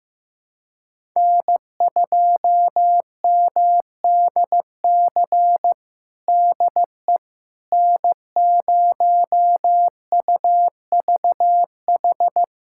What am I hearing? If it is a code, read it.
Morse code, heard as a keyed tone, about 16 wpm, 700 Hz. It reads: N2MDC DE N0UVH
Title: USA Callsign DE Callsign - 5 Characters - 15wpm